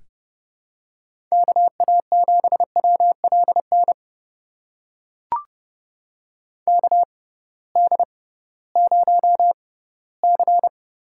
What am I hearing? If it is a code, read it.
Morse code, heard as a keyed tone, about 32 wpm, spaced as for 18 wpm, 700 Hz. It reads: KA7WLN KD0C
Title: USA Call Signs - 6x Character Spacing - 30wpm